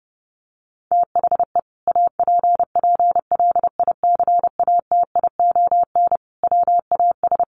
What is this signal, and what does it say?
Morse code, heard as a keyed tone, about 30 wpm, 700 Hz. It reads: THE APPLICATION WAS
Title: Sentences from Top 1200 Words 30wpm